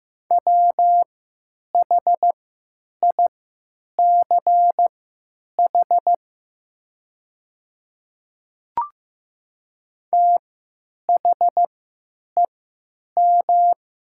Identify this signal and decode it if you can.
Morse code, heard as a keyed tone, about 15 wpm, spaced as for 9 wpm, 700 Hz. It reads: WHICH THEM
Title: 3x Character Spacing - Top 300 Words - 15wpm